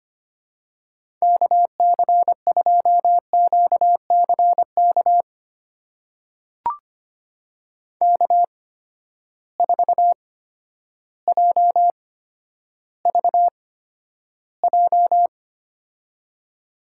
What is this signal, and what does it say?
Morse code, heard as a keyed tone, about 25 wpm, spaced as for 11 wpm, 700 Hz. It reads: KC2QCK K4JVJ
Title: USA Call Signs - 8x Character Spacing - 25wpm